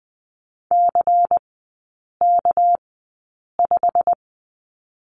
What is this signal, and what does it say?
Morse code, heard as a keyed tone, about 20 wpm, spaced as for 8 wpm, 700 Hz. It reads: CK5
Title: 198 Lesson 32e - Letter k - Three Characters 20wpm